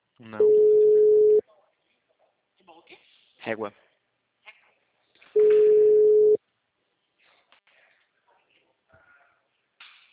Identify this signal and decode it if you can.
Ringback tone (Europe, 425 Hz), with speech.
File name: out- -2013- - - .51125